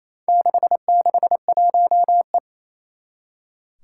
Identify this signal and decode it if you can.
Morse code, heard as a keyed tone, about 28 wpm, 700 Hz. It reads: 661E
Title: Four Challenging Letter-Number Combinations 28wpm